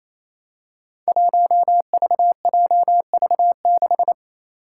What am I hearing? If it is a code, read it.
Morse code, heard as a keyed tone, about 27 wpm, 700 Hz. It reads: 1VJV6
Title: Five Challenging Letter-Number Combinations 28wpm